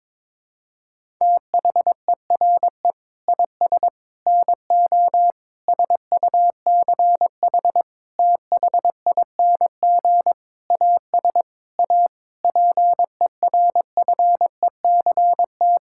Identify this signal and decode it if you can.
Morse code, heard as a keyed tone, about 22 wpm, 700 Hz. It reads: THERE IS NO SUCH THING AS A PERFECT
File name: Sentences with 7-Word Phrases - 22wpm